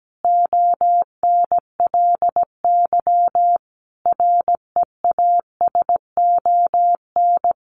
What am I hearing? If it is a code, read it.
Morse code, heard as a keyed tone, about 17 wpm, 700 Hz. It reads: ONLY REASON
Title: Sentences from Top 700 Words 17wpm